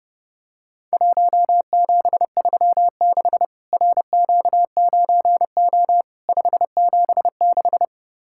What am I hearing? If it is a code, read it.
Morse code, heard as a keyed tone, about 30 wpm, 700 Hz. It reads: 1736 RQ9O 576
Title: Worked All Europe DX Contest - QTC Exchange - 30wpm